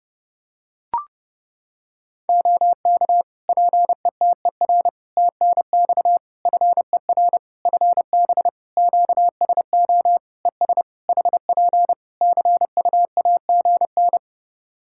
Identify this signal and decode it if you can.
Morse code, heard as a keyed tone, about 30 wpm, 700 Hz. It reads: OK PETER TNX FER FB QSO ES HP CUAGN
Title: Practice QSO - Exchange 4 - Ending - 30wpm